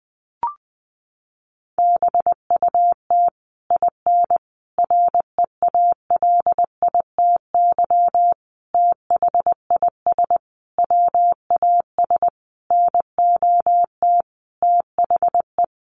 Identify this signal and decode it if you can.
Morse code, heard as a keyed tone, about 20 wpm, 700 Hz. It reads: BUT IN REALITY THIS WAS NOT THE
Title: Sentences from Top 1200 Words 20wpm